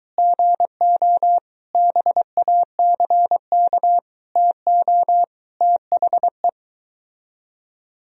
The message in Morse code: GO BACK TO THE

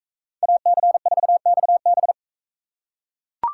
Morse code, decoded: AC4XB

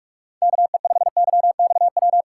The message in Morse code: KE5YXW

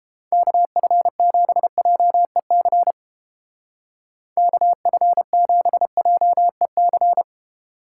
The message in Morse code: KF7JEC KF7JEC

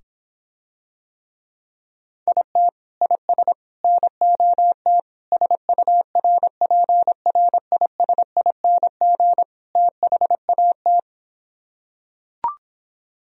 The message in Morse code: IT IS NOT SURPRISING THAT

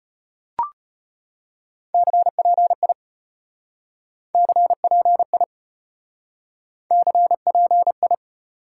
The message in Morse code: CPI CPI CPI

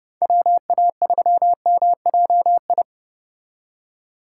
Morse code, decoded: WA3MJI

